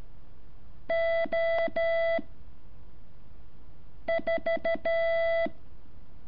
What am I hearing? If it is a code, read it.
Morse code, heard as a keyed tone, about 12 wpm, 660 Hz. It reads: O 4
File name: curso / ejercicio 8